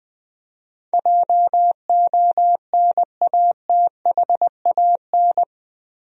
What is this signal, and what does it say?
Morse code, heard as a keyed tone, about 20 wpm, 700 Hz. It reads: JONATHAN
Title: USA Common Names - 20wpm